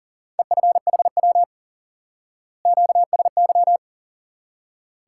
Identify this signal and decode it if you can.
Morse code, heard as a keyed tone, about 40 wpm, 700 Hz. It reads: EFHW QSY
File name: IQE Course 3 - Lesson 89 - YAGI - Two Elements - 40wpm